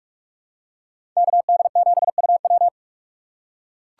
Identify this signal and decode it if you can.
Morse code, heard as a keyed tone, about 43 wpm, 700 Hz. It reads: KD7UW
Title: USA Call Signs - 8x Character Spacing - 45wpm